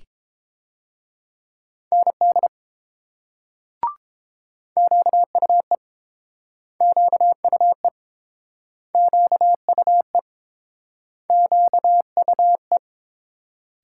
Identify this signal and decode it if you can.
Morse code, heard as a keyed tone, about 31 wpm, 700 Hz. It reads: ND QUE QUE QUE QUE